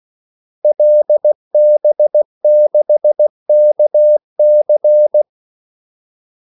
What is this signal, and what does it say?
Morse code, heard as a keyed tone, about 16 wpm, 600 Hz. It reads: LB6KC
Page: Kaldesignaler 16 wpm